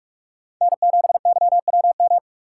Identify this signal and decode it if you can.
Morse code, heard as a keyed tone, about 46 wpm, 700 Hz. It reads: N7YWM
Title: USA Call Signs - 5 Characters - 6x Character Spacing - 45wpm